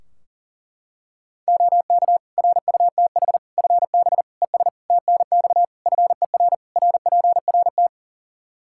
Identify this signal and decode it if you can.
Morse code, heard as a keyed tone, about 40 wpm, 700 Hz. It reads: OK RUTH FB ES TNX FER RPRT
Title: Practice QSO - Exchange 2 - Info - 40wpm